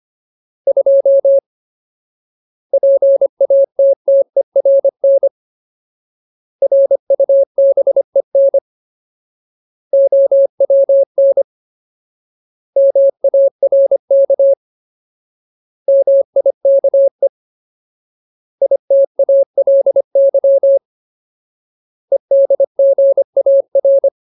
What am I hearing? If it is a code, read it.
Morse code, heard as a keyed tone, about 25 wpm, 550 Hz. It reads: 2 PATTERN RUBEN OWN MARK MIKE ITALY EDGAR
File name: Ultimate Word List / 25WPM Ultimate